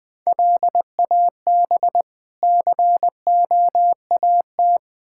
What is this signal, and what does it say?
Morse code, heard as a keyed tone, about 21 wpm, 700 Hz. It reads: LAB COAT
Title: Difficult Words - Speed-Racing with Hints - 20wpm